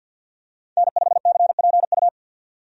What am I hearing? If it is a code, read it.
Morse code, heard as a keyed tone, about 50 wpm, 700 Hz. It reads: N5CPU
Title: USA Call Signs - 5 Characters - 50wpm